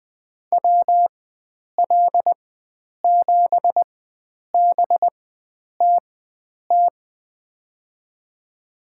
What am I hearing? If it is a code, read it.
Morse code, heard as a keyed tone, about 20 wpm, spaced as for 9 wpm, 700 Hz. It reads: WL7BTT
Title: USA Call Signs - 6 Characters - 4x Character Spacing - 20wpm